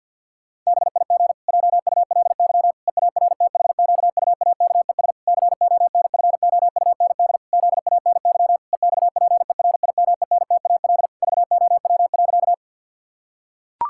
Morse code, heard as a keyed tone, about 51 wpm, 700 Hz. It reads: BIG JULY EARTHQUAKES CONFOUND ZANY EXPERIMENTAL VOW.
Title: Pangram Challenge v2 - 50wpm